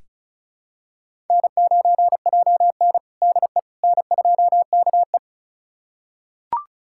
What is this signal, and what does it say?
Morse code, heard as a keyed tone, about 36 wpm, 700 Hz. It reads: N9JN DE N2KE
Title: USA Callsign DE Callsign - 4 Characters - 35wpm